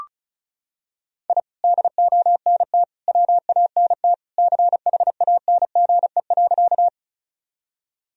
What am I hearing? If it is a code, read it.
Morse code, heard as a keyed tone, about 36 wpm, 700 Hz. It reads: I DONT WANT CHANGE.
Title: Sentences from Top 1000 Words - Review Part 1 - 35wpm